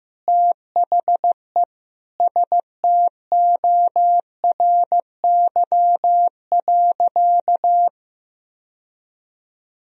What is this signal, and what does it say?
Morse code, heard as a keyed tone, about 16 wpm, 700 Hz. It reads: THE STORY.
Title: Sentences from Top 1000 Words - Review Part 2 - 15wpm